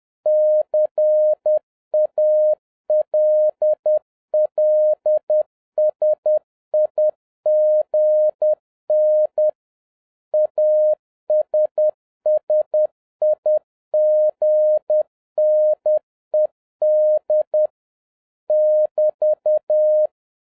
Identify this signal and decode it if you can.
Morse code, heard as a keyed tone, about 10 wpm, 610 Hz. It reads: CALLSIGN ASSIGNED <BT>
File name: 2011-10-mp3 / n01-CALLSIGN ASSIGNED 10wpm